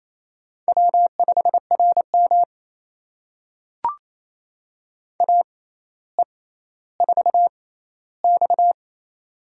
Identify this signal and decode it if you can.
Morse code, heard as a keyed tone, about 27 wpm, spaced as for 14 wpm, 700 Hz. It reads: W5RM AE4X
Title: USA Call Signs - 4 Characters - 6x Character Spacing - 28wpm